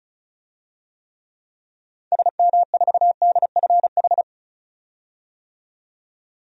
Morse code, heard as a keyed tone, about 36 wpm, 690 Hz. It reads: SM4DFH
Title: DXCC-100 Call Signs - 6 Characters - 35wpm